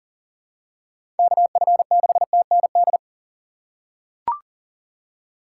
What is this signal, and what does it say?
Morse code, heard as a keyed tone, about 39 wpm, 700 Hz. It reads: KF6TND
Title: USA Call Signs - 6 Characters - 3x Character Spacing - 40wpm